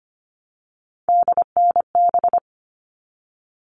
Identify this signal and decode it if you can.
Morse code, heard as a keyed tone, about 25 wpm, 700 Hz. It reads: DNB
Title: All Cut Numbers - 3-Digit 25wpm